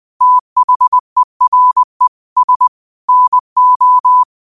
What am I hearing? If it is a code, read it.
Morse code, heard as a keyed tone, about 20 wpm, 1000 Hz. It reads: THERE S NO
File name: 20wpm